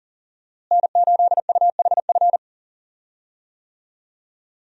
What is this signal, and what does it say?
Morse code, heard as a keyed tone, about 39 wpm, 700 Hz. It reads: N8UHF
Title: USA Call Signs - 5 Characters - Repeated 2x - 40wpm